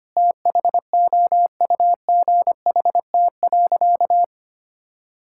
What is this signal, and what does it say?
Morse code, heard as a keyed tone, about 25 wpm, 700 Hz. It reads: THOUGHT.
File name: practice / 25wpm / Sentences with 2-Word Phrases - 25wpm